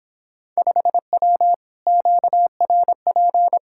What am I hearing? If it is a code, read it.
Morse code, heard as a keyed tone, about 26 wpm, 700 Hz. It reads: 5W QRP